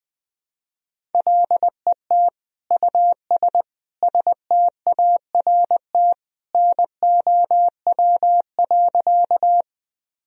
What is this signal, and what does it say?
Morse code, heard as a keyed tone, about 20 wpm, 700 Hz. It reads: LET US START NOW.
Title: Sentences from Top 1000 Words - Review Part 1 - 20wpm